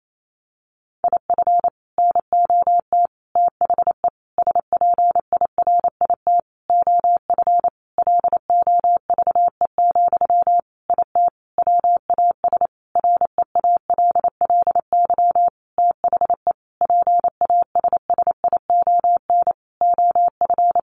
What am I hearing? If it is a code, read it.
Morse code, heard as a keyed tone, about 28 wpm, 700 Hz. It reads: IF NOT THE SPIRIT OF LOVE, IT WAS REALLY THE PASSION OF
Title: Sentences from Top 1400 Words 28wpm